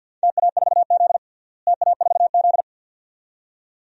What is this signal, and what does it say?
Morse code, heard as a keyed tone, about 46 wpm, 700 Hz. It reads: TA4Z TA4Z
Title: DXCC-100 Call Signs - 4 Characters - Repeated 2x - 50wpm